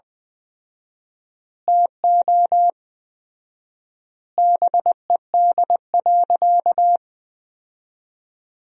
Morse code, heard as a keyed tone, about 20 wpm, 700 Hz. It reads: TO BED.
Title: 4x Word-Spacing - Sentences from Top 500 Words - 20wpm